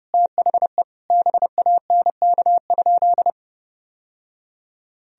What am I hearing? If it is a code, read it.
Morse code, heard as a keyed tone, about 30 wpm, 700 Hz. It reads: THE BANK?
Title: Sentences from Top 1000 Words - Review Part 4 - 30wpm